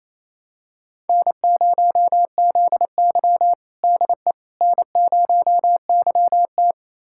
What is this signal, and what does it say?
Morse code, heard as a keyed tone, about 28 wpm, 700 Hz. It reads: N0ZY DE N0YT
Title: USA Callsign DE Callsign - 4 Characters - 28wpm